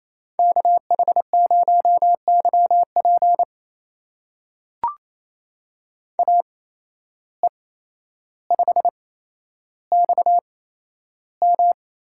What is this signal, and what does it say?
Morse code, heard as a keyed tone, about 27 wpm, spaced as for 12 wpm, 700 Hz. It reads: KH0YP AE5XM
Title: USA Call Signs - 5 Characters - 8x Character Spacing - 28wpm